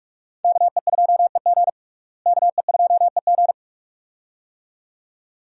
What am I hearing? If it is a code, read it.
Morse code, heard as a keyed tone, about 42 wpm, 700 Hz. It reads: KE2EG KE2EG
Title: USA Call Signs - 5 Characters - Repeated 2x - 45wpm